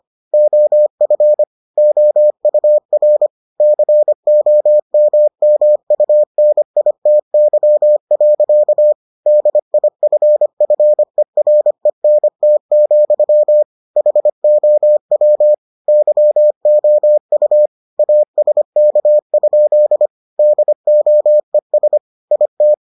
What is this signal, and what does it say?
Morse code, heard as a keyed tone, about 25 wpm, 600 Hz. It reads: OF OUR COMMUNITY. DIFFERENT, HOW YOU ASK? DOES IT